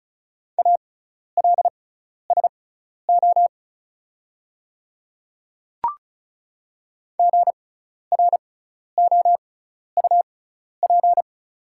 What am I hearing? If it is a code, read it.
Morse code, heard as a keyed tone, about 36 wpm, spaced as for 12 wpm, 690 Hz. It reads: ALSO GROUP
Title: 6x Character Spacing - Top 500 Words - 35wpm